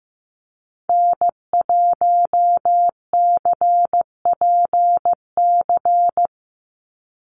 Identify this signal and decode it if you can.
Morse code, heard as a keyed tone, about 15 wpm, 700 Hz. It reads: N1CPC
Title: USA Call Signs 15wpm